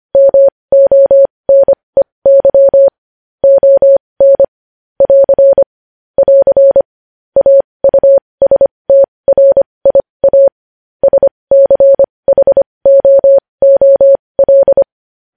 Morse code, heard as a keyed tone, about 25 wpm, spaced as for 20 wpm, 550 Hz. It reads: MONEY ON <AR> <AR> AUSTRIA SCHOOL
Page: news-headlines-in-morse-code-at-20-wpm